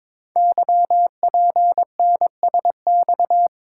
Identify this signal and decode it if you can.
Morse code, heard as a keyed tone, about 22 wpm, 690 Hz. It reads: YPNSX
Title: Code Groups - 5 Letter:Numbers - 22wpm